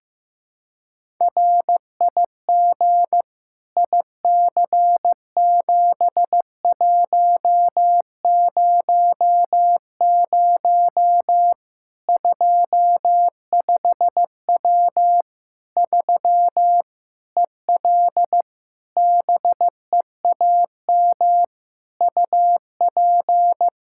Morse code, heard as a keyed tone, about 15 wpm, 700 Hz. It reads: RIG IC7100 25W 3 EL BEAM UP
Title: Practice QSO - Exchange 2 - Info - 15wpm